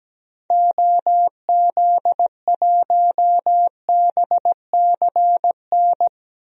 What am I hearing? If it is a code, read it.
Morse code, heard as a keyed tone, about 17 wpm, 700 Hz. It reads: OZ1BCN
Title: DXCC-100 Call Signs - 6 Characters - Repeated 3x - 17wpm